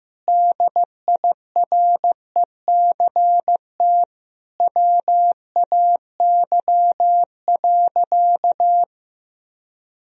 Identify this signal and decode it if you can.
Morse code, heard as a keyed tone, about 15 wpm, 700 Hz. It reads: DIRECT WAY.
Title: Sentences from Top 1000 Words - Review Part 2 - 15wpm